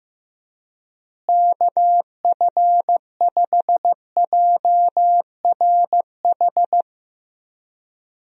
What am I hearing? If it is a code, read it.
Morse code, heard as a keyed tone, about 15 wpm, 700 Hz. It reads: KF5JRH
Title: USA Call Signs - 6 Characters - Repeated 2x - 15wpm